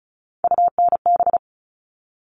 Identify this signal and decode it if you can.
Morse code, heard as a keyed tone, about 34 wpm, 700 Hz. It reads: UNB
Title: All Cut Numbers - 3-Digit 35wpm